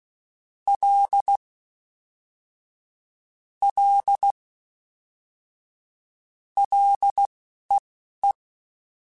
Morse code, heard as a keyed tone, about 16 wpm, spaced as for 11 wpm, 800 Hz. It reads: L L LEE